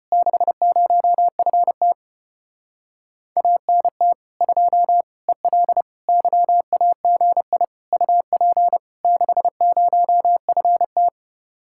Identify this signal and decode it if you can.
Morse code, heard as a keyed tone, about 33 wpm, 700 Hz. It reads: 60FT ANT 2 EL YAGI UP 60FT